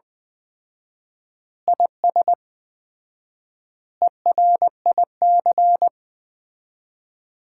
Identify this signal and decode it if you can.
Morse code, heard as a keyed tone, about 20 wpm, 700 Hz. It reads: IS ERIC
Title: Common QSO Phrases - 4x Word-Spacing - 20wpm